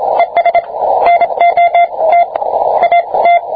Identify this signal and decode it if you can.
Morse code, heard as a keyed tone, about 25 wpm, spaced as for 22 wpm, 690 Hz, with radio noise. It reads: ES NOT AT